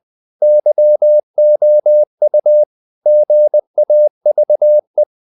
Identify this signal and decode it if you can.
Morse code, heard as a keyed tone, about 20 wpm, 600 Hz. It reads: YOU GAVE
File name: CW / Practice / If20